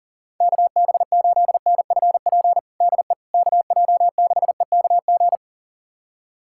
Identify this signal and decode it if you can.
Morse code, heard as a keyed tone, about 40 wpm, 700 Hz. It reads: KB8NFP DE KJ6EKG
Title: USA Callsign DE Callsign - 6 Characters - 40wpm